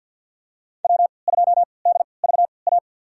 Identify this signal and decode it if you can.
Morse code, heard as a keyed tone, about 50 wpm, spaced as for 28 wpm, 700 Hz. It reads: W2DVA
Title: USA Call Signs - 5 Characters - 3x Character Spacing - 50wpm